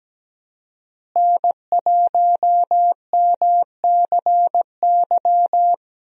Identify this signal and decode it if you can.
Morse code, heard as a keyed tone, about 17 wpm, 700 Hz. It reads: N1MCY
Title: USA Call Signs - 5 Characters - Repeated 3x - 17wpm